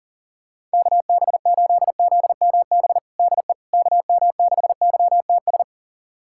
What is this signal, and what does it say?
Morse code, heard as a keyed tone, about 40 wpm, 700 Hz. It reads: KB8ZMB DE KM6YTS
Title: USA Callsign DE Callsign - 6 Characters - 40wpm